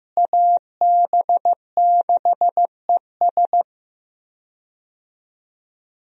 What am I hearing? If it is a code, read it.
Morse code, heard as a keyed tone, about 15 wpm, 700 Hz. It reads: AB6ES
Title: USA Call Signs - 5 Characters - Repeated 2x - 15wpm